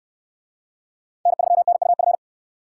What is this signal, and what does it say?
Morse code, heard as a keyed tone, about 69 wpm, 700 Hz. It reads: N3NUF